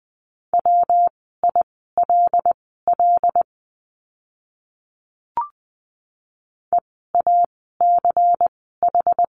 Morse code, heard as a keyed tone, about 20 wpm, spaced as for 15 wpm, 700 Hz. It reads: WILL EACH
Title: 192 Lesson 31f - Letter v - Words 20wpm